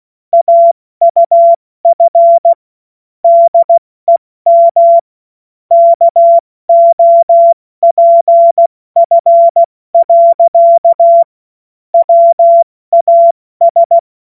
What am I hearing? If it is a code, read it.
Morse code, heard as a keyed tone, about 16 wpm, 680 Hz. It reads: AUF DEM KOPF. WAS
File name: Morsekurs / Witze / Witz 19 März 2024 70 F80